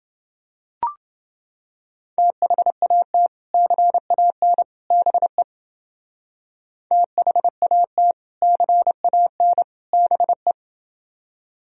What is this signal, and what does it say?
Morse code, heard as a keyed tone, about 31 wpm, 700 Hz. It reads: THAT CAN BE THAT CAN BE